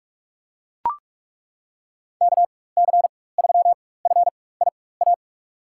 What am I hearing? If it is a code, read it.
Morse code, heard as a keyed tone, about 42 wpm, spaced as for 21 wpm, 700 Hz. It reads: KC3FIA
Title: USA Call Signs - 6 Characters - 4x Character Spacing - 45wpm